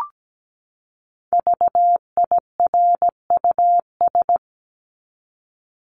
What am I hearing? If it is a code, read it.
Morse code, heard as a keyed tone, about 17 wpm, 700 Hz. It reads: VIRUS
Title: Top 1251-1500 Words 17wpm